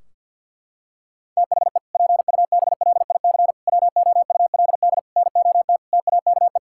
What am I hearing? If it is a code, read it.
Morse code, heard as a keyed tone, about 52 wpm, 700 Hz. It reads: THE PUBLIC WOULD NOT TAKE
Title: Sentences from Top 1600 Words 50wpm